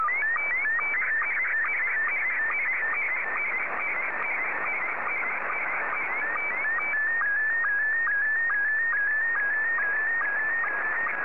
SSTV (slow-scan television), Scottie 1.